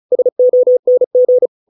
Morse code, heard as a keyed tone, about 35 wpm, 500 Hz. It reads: SONG